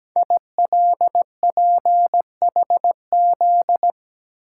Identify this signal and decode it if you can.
Morse code, heard as a keyed tone, about 17 wpm, 700 Hz. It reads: ILPHZ
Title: Code Groups - 5 Letter:Numbers - 17wpm